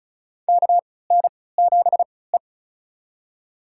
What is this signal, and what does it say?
Morse code, heard as a keyed tone, about 34 wpm, spaced as for 20 wpm, 700 Hz. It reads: KN7E
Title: USA Call Signs - 3x Character Spacing - 35wpm